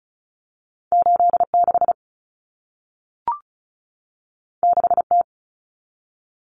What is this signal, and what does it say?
Morse code, heard as a keyed tone, about 36 wpm, 700 Hz. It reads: 86 6T
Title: Common Cut Numbers - 2-Digit 35wpm